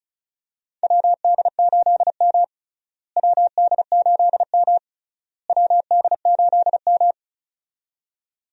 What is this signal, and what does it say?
Morse code, heard as a keyed tone, about 36 wpm, 700 Hz. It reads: WD8M WD8M WD8M
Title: USA Call Signs - 4 Characters - Repeated 3x - 35wpm